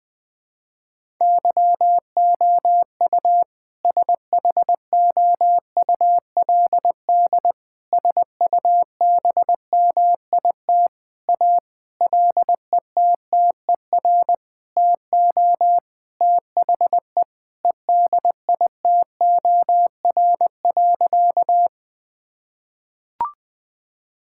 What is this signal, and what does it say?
Morse code, heard as a keyed tone, about 21 wpm, 700 Hz. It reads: YOU SHOULD SUBMIT A LETTER TO THE EDITOR.
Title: Sentences with 6-Word Phrases - 20wpm